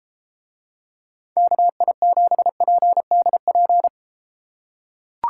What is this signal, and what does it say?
Morse code, heard as a keyed tone, about 29 wpm, 700 Hz. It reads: KI7PDP